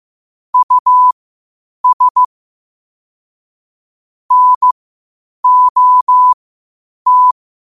Morse code, heard as a keyed tone, about 15 wpm, spaced as for 8 wpm, 1000 Hz. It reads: US NOT
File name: qotdfiles / 8wpm